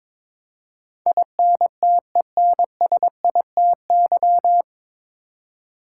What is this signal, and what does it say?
Morse code, heard as a keyed tone, about 22 wpm, 700 Hz. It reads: INTENSITY
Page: Morse Code Ninja practice 22wpm